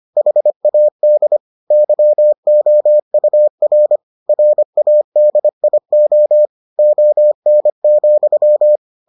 Morse code, heard as a keyed tone, about 25 wpm, 600 Hz. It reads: HAD YOUR RADIO ON,